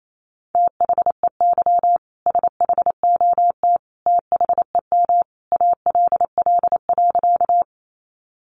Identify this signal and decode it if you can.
Morse code, heard as a keyed tone, about 28 wpm, 700 Hz. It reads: THEY SHOT THEM ALL.
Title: Sentences from Top 1100 Words 28wpm